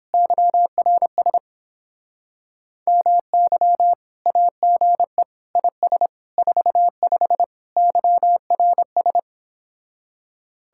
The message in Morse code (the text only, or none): YRS MY AGE IS 45 YRS